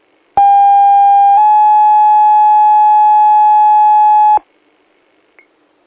Two-tone page, heard as a one-second tone, then a three-second tone, about 800 then 850 Hz. It is heard from 0.4 to 4.4 s.